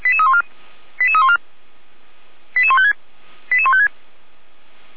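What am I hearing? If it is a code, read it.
Five-tone selective call (ZVEI), heard as quick runs of five tones, tones of about 70 ms. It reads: 88314 88156 88156
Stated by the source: Five-tone ZVEI selective call received on SDR.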